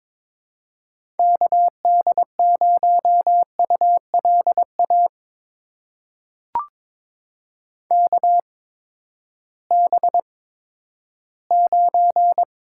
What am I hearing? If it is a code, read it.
Morse code, heard as a keyed tone, about 23 wpm, spaced as for 13 wpm, 700 Hz. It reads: KD0ULA KB9
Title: USA Call Signs - 6 Characters - 8x Character Spacing - 22wpm